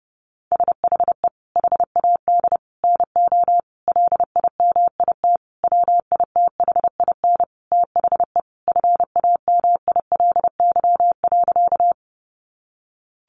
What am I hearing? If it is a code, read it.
Morse code, heard as a keyed tone, about 30 wpm, 700 Hz. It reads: SHE HAD NO LIMIT WITHIN THE FAMILY.
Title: Sentences from Top 1500 Words 30wpm